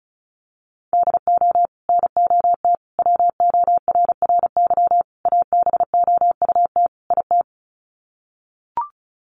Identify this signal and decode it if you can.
Morse code, heard as a keyed tone, about 34 wpm, 700 Hz. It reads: DO NOT WORRY ABOUT IT
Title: Sets of 5 Words 35wpm